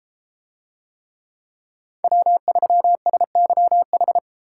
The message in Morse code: W3SYH

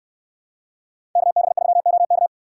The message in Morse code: KB3OK

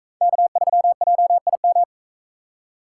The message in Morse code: K3JIM

